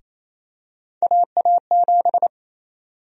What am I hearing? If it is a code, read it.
Morse code, heard as a keyed tone, about 28 wpm, 700 Hz. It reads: AA7
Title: USA Call Sign Prefixes - 28wpm